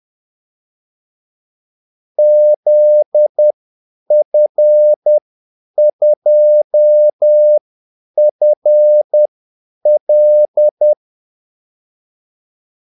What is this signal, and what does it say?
Morse code, heard as a keyed tone, about 10 wpm, spaced as for 8 wpm, 600 Hz. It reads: ZF2FL
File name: Callsigns-10-8wpm